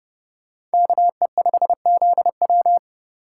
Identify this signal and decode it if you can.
Morse code, heard as a keyed tone, about 28 wpm, 700 Hz. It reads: KE5ZW